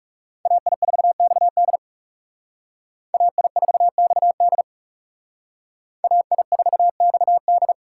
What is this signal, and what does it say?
Morse code, heard as a keyed tone, about 40 wpm, 700 Hz. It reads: AI4XD AI4XD AI4XD